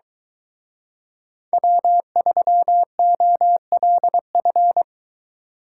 Morse code, heard as a keyed tone, about 24 wpm, 700 Hz. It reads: W3OLF